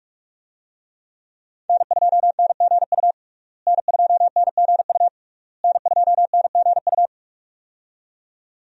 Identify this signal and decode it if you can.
Morse code, heard as a keyed tone, about 46 wpm, 700 Hz. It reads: N2NGU N2NGU N2NGU
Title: USA Call Signs - 5 Characters - Repeated 3x - 45wpm